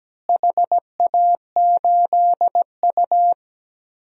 Morse code, heard as a keyed tone, about 17 wpm, 700 Hz. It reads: HA8U
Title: DXCC-100 Call Signs - 4 Characters - Repeated 3x - 17wpm